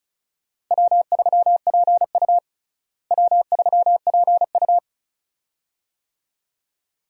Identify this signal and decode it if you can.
Morse code, heard as a keyed tone, about 36 wpm, 700 Hz. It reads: W3PU W3PU
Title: USA Call Signs - 4 Characters - Repeated 2x - 35wpm